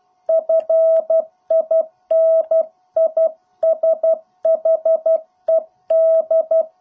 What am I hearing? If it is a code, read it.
Morse code, heard as a keyed tone, about 12 wpm, 650 Hz. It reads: FINISHED